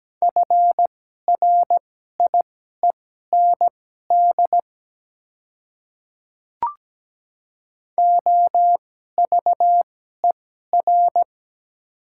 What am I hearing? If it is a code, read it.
Morse code, heard as a keyed tone, about 17 wpm, spaced as for 12 wpm, 700 Hz. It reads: FRIEND OVER
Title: 2x Character Spacing - Top 300 Words - 17wpm